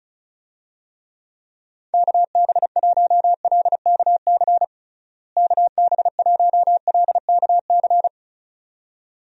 Morse code, heard as a keyed tone, about 34 wpm, 700 Hz. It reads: KB1LKC KB1LKC
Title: USA Call Signs - 6 Characters - Repeated 2x - 35wpm